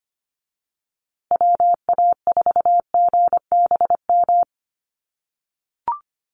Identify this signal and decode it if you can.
Morse code, heard as a keyed tone, about 25 wpm, 700 Hz. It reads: WA4GBM
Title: USA Call Signs Repeated 3x 25wpm